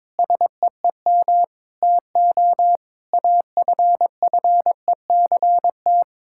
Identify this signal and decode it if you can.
Morse code, heard as a keyed tone, about 22 wpm, 700 Hz. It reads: SEEM TO AFFECT
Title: Sentences from Top 1000 Words - Review Part 6 - 22wpm